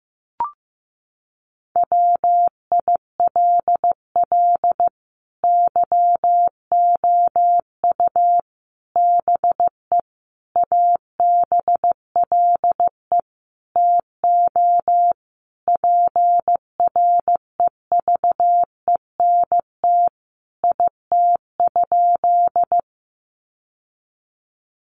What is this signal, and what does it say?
Morse code, heard as a keyed tone, about 15 wpm, 700 Hz. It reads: WILL YOU BE ABLE TO PREVENT IT?
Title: Sentences from Top 1100 Words 15wpm